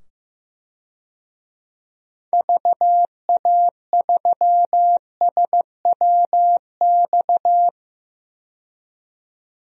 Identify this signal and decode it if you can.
Morse code, heard as a keyed tone, about 15 wpm, 700 Hz. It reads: VA3SWX
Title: DXCC-100 Call Signs - 6 Characters - 15wpm